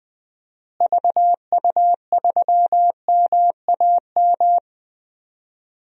Morse code, heard as a keyed tone, about 21 wpm, 700 Hz. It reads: VU3MAM
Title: DXCC-100 Call Signs - 6 Characters - Repeated 2x - 20wpm